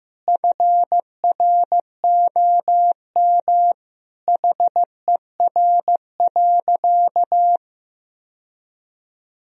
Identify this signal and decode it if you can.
Morse code, heard as a keyed tone, about 15 wpm, 700 Hz. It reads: FROM HER.
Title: Sentences from Top 1000 Words - Review Part 2 - 15wpm